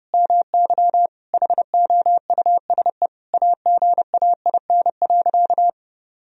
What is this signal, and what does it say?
Morse code, heard as a keyed tone, about 32 wpm, 700 Hz. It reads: MY HOUSE AGAIN.
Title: Sentences from Top 1000 Words - Review Part 1 - 30wpm